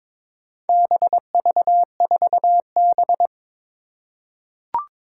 Morse code, heard as a keyed tone, about 22 wpm, 700 Hz. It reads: BV4B
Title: Four Challenging Letter-Number Combinations 22wpm